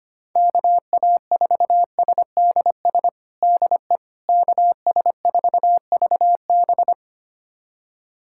Morse code, heard as a keyed tone, about 25 wpm, 700 Hz. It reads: KA4SDS DE KS4VB
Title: USA Callsign DE Callsign 25wpm